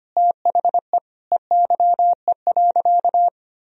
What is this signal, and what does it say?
Morse code, heard as a keyed tone, about 25 wpm, 700 Hz. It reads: THE EYE.
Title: Sentences from Top 1000 Words - Review Part 1 - 25wpm